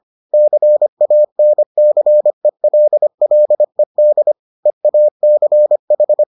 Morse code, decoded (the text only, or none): CANCELLED EACH